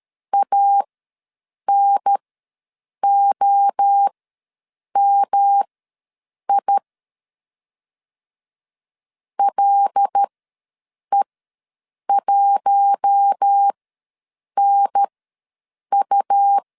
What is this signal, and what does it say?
Morse code, heard as a keyed tone, about 13 wpm, spaced as for 7 wpm, 800 Hz. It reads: ANOMI LE1NU
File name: LEZIONI / 5 LBVC / 5 L ws 12wpm 2958